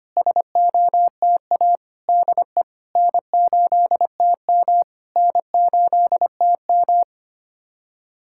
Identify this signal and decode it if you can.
Morse code, heard as a keyed tone, about 25 wpm, 700 Hz. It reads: SOTA DE N8TM N8TM
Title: SOTA QSO Practice 25wpm